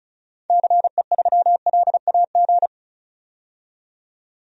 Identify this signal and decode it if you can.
Morse code, heard as a keyed tone, about 36 wpm, 700 Hz. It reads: CE3LAG
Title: DXCC-100 Call Signs - 6 Characters - 35wpm